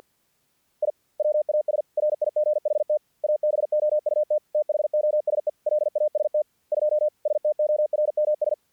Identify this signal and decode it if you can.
Morse code, heard as a keyed tone, about 48 wpm, 600 Hz. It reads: I WAS RIGHT ABOUT THOSE LAST 2 STORMS
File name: archived / CRSnet